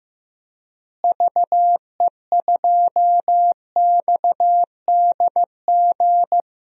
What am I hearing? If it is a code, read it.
Morse code, heard as a keyed tone, about 15 wpm, 700 Hz. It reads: VE2XDG
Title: DXCC-100 Call Signs - 6 Characters - Repeated 3x - 15wpm